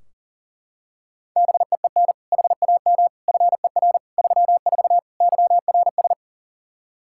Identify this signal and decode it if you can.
Morse code, heard as a keyed tone, about 40 wpm, 700 Hz. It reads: BEEN HAM FER 34 YRS
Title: Practice QSO - Exchange 3 - More Info - 40wpm